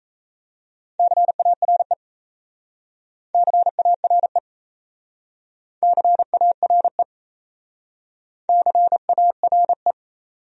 Morse code, heard as a keyed tone, about 38 wpm, 700 Hz. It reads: CARE CARE CARE CARE